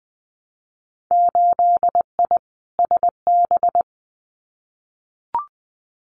Morse code, heard as a keyed tone, about 19 wpm, 700 Hz. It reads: 8I SB
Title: Winter Field Day Exchanges Encore 20wpm